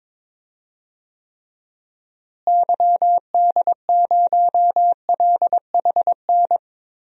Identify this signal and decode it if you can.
Morse code, heard as a keyed tone, about 22 wpm, 700 Hz. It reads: YD0LHN
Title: DXCC-100 Call Signs - 6 Characters - Repeated 2x - 22wpm